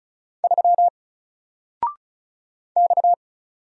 Morse code, heard as a keyed tone, about 35 wpm, 700 Hz. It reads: 3 X
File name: Single-Letter-Number-35wpm